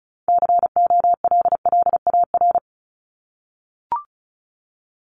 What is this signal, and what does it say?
Morse code, heard as a keyed tone, about 34 wpm, 700 Hz. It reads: COLLAR
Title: Top 1751-2000 Words 35wpm